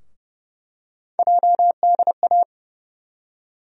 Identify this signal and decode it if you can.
Morse code, heard as a keyed tone, about 32 wpm, 700 Hz. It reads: JDA
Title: USA Call Sign Postfixes - 3 Letters 30wpm